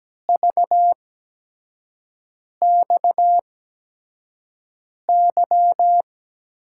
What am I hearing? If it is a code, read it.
Morse code, heard as a keyed tone, about 17 wpm, spaced as for 5 wpm, 700 Hz. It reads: VXY
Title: USA Call Signs - 5 Characters - 8x Character Spacing - 17wpm